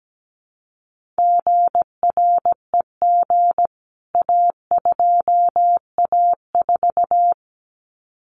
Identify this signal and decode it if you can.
Morse code, heard as a keyed tone, about 17 wpm, 700 Hz. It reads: GREG A2A4
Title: CWT Exchange 17wpm